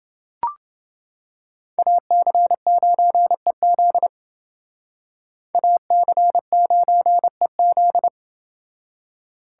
Morse code, heard as a keyed tone, about 28 wpm, 700 Hz. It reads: AC9EZ AC9EZ